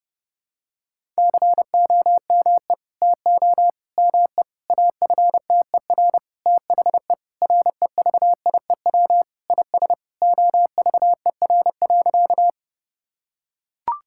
Morse code, heard as a keyed tone, about 30 wpm, 700 Hz. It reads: COME TO ME AFTER THE REVIEW IS OVER.
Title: Sentences from Top 1000 Words - Review Part 6 - 30wpm